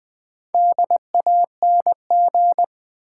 Morse code, heard as a keyed tone, about 20 wpm, 690 Hz. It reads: DANG
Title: Build A Word - 2 Letters At A Time - 6-Letter Words - 20wpm